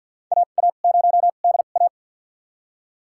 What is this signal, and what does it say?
Morse code, heard as a keyed tone, about 47 wpm, spaced as for 35 wpm, 700 Hz. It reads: AA0DA
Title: USA Call Signs - 5 Characters - 2x Character Spacing - 50wpm